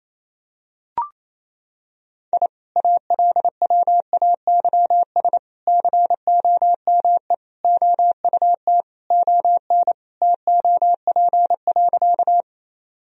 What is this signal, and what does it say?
Morse code, heard as a keyed tone, about 28 wpm, 700 Hz. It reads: I ALWAYS COME OUT ON TOP.
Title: Sentences from Top 1000 Words - Review Part 2 - 28wpm